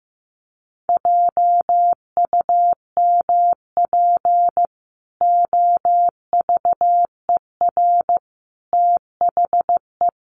Morse code, heard as a keyed tone, about 15 wpm, 700 Hz. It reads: JUMP OVER THE
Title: Sentences from Top 1300 Words 15wpm